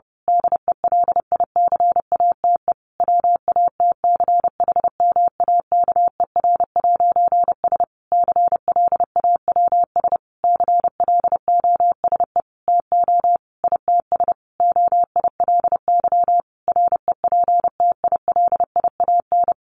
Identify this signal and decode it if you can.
Morse code, heard as a keyed tone, about 30 wpm, 700 Hz. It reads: DELICATE WATCHMAKER'S CLAWS CLOSE TO ITS OILY REPTILIAN
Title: A Sound Of Thunder 30wpm